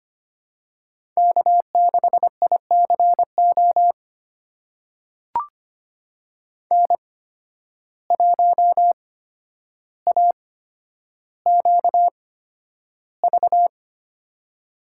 Morse code, heard as a keyed tone, about 25 wpm, spaced as for 11 wpm, 700 Hz. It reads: K6ICO N1AQV
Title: USA Call Signs - 5 Characters - 8x Character Spacing - 25wpm